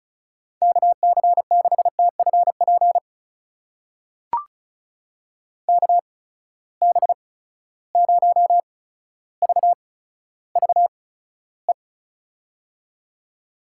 Morse code, heard as a keyed tone, about 34 wpm, spaced as for 15 wpm, 700 Hz. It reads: KC6TFP KB0VVE
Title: USA Call Signs - 6 Characters - 8x Character Spacing - 35wpm